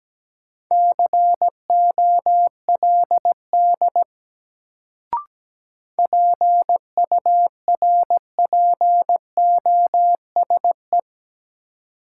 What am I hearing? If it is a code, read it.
Morse code, heard as a keyed tone, about 17 wpm, 700 Hz. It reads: COLD PURPOSE
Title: Sets of 5 Words - New Word Drill - 17wpm